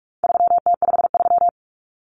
Morse code, heard as a keyed tone, about 44 wpm, 700 Hz. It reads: 3T53
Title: Common Cut Numbers - 4-Digit 45wpm